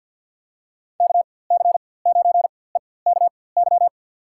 Morse code, heard as a keyed tone, about 52 wpm, spaced as for 24 wpm, 700 Hz. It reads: KC9EKY
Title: USA Call Signs - 6 Characters - 4x Character Spacing - 50wpm